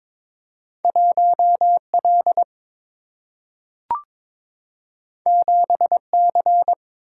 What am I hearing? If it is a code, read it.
Morse code, heard as a keyed tone, about 22 wpm, 700 Hz. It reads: 1 L 7 C
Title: Single-Letter Single-Number 22wpm